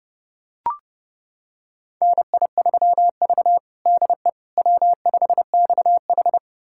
Morse code, heard as a keyed tone, about 32 wpm, 700 Hz. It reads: NI3V DE W5XH
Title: USA Callsign DE Callsign - 4 Characters - 30wpm